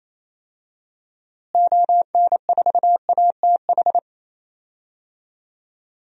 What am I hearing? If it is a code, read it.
Morse code, heard as a keyed tone, about 28 wpm, 700 Hz. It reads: ON4ATH
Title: DXCC-100 Call Signs - 6 Characters - Repeated 2x - 28wpm